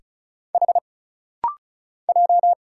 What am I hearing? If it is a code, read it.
Morse code, heard as a keyed tone, about 36 wpm, 700 Hz. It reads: H J
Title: Single Letters - Rapid-Fire 35wpm